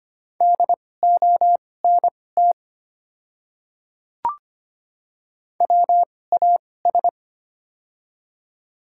Morse code, heard as a keyed tone, about 25 wpm, spaced as for 18 wpm, 700 Hz. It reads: DONT WAS
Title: 2x Character Spacing - Top 300 Words - 25wpm